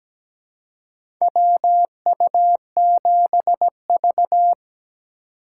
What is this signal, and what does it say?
Morse code, heard as a keyed tone, about 17 wpm, 700 Hz. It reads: WU7V
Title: USA Call Signs - 4 Characters - 3x Character Spacing - 17wpm